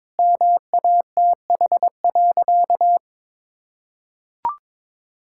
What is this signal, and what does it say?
Morse code, heard as a keyed tone, about 22 wpm, 700 Hz. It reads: MATH.
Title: Sentences from Top 1000 Words - Review Part 6 - 22wpm